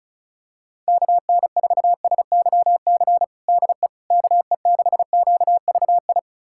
Morse code, heard as a keyed tone, about 34 wpm, 700 Hz. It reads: KN4SYC DE KE6QVI
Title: USA Callsign DE Callsign - 6 Characters - 35wpm